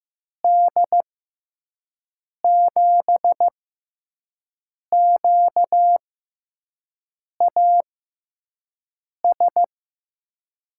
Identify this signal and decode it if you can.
Morse code, heard as a keyed tone, about 15 wpm, spaced as for 5 wpm, 700 Hz. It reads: D7QAS
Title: USA Call Signs - 6 Characters - 6x Character Spacing - 15wpm